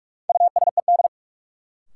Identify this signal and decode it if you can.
Morse code, heard as a keyed tone, about 46 wpm, 700 Hz. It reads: USED
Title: Sentences from Top 600 Words - New Word Drill - 45wpm